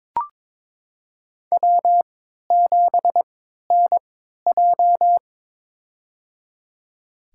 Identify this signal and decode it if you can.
Morse code, heard as a keyed tone, about 22 wpm, spaced as for 13 wpm, 700 Hz. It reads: W7NJ
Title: USA Call Signs - 3x Character Spacing - 22wpm